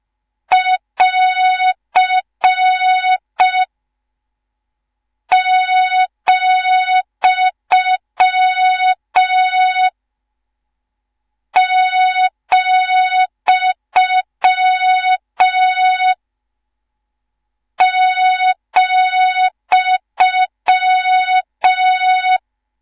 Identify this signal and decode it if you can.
Morse code, heard as a keyed tone, about 5 wpm, 3700 Hz. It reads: <AR> , , ,